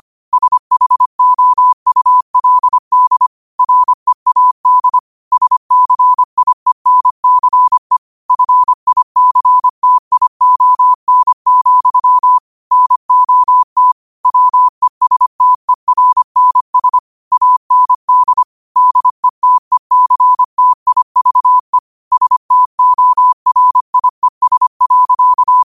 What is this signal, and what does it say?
Morse code, heard as a keyed tone, about 25 wpm, 1000 Hz. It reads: SHOULD READ SCIENCE FICTION, NOT WESTERNS AND DETECTIVE STORIES.
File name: qotdfiles / 25wpm